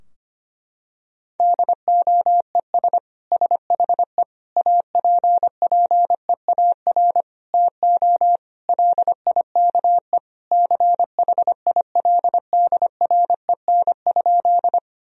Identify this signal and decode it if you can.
Morse code, heard as a keyed tone, about 25 wpm, 700 Hz. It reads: DOES SHE APPEAR TO LIKE CHILDREN?
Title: Sentences from Top 1000 Words - Review Part 2 - 25wpm